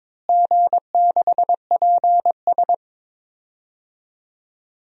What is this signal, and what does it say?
Morse code, heard as a keyed tone, about 22 wpm, 700 Hz. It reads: G6PS
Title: DXCC-100 Call Signs - 4 Characters - Repeated 3x - 22wpm